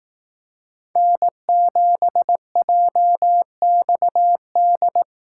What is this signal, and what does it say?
Morse code, heard as a keyed tone, about 18 wpm, 700 Hz. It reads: N7JXD
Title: Speed Racing - USA 1x3 Call Signs 15wpm